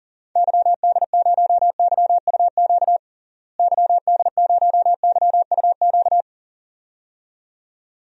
Morse code, heard as a keyed tone, about 37 wpm, 700 Hz. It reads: YD0YUQ YD0YUQ
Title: DXCC-100 Call Signs - 6 Characters - Repeated 2x - 40wpm